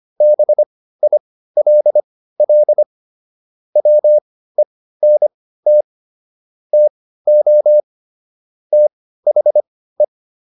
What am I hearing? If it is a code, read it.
Morse code, heard as a keyed tone, about 25 wpm, spaced as for 15 wpm, 600 Hz. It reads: BILL WENT TO THE